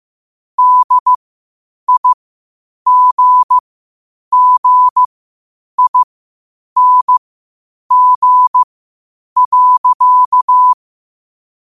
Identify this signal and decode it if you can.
Morse code, heard as a keyed tone, about 15 wpm, spaced as for 8 wpm, 1000 Hz. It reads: DIGGING.
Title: 8 WPM morse code quotes for Thu, 29 Feb 2024